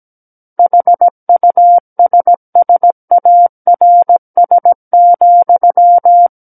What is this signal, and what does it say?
Morse code, heard as a keyed tone, about 17 wpm, 700 Hz. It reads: HUSSARS,
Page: SKCC Learning Center - Morse Code Practice Files